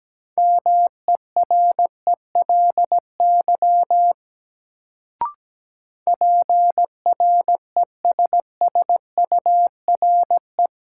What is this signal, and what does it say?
Morse code, heard as a keyed tone, about 17 wpm, 700 Hz. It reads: MERELY PRESSURE
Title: Sentences from Top 1400 Words - New Word Drill Alt - 17wpm